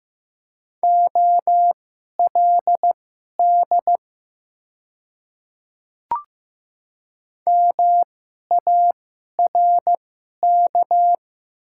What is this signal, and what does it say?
Morse code, heard as a keyed tone, about 15 wpm, spaced as for 11 wpm, 700 Hz. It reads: OLD MARK
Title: 2x Character Spacing - Top 300 Words - 15wpm